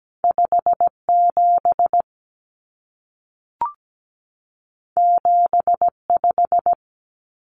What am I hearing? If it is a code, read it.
Morse code, heard as a keyed tone, about 17 wpm, 700 Hz. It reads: 57 75
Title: Common Cut Numbers - 2-Digit 17wpm